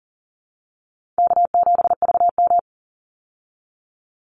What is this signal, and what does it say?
Morse code, heard as a keyed tone, about 40 wpm, 700 Hz. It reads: K7VM
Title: Call Signs Repeated 2x - 40wpm